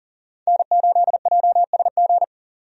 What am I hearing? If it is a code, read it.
Morse code, heard as a keyed tone, about 40 wpm, 700 Hz. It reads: N8JSG
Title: USA Call Signs - 5 Characters - 40wpm